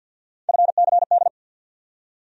Morse code, heard as a keyed tone, about 47 wpm, 700 Hz. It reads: FCD
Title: USA Call Sign Postfixes - 3 Letters 50wpm